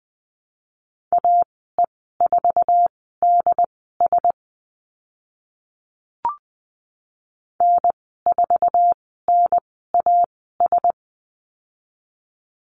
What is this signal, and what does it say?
Morse code, heard as a keyed tone, about 20 wpm, spaced as for 15 wpm, 700 Hz. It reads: AE4DS N4NAS
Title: 076 Lesson 13g - Letter L - Call Signs 20wpm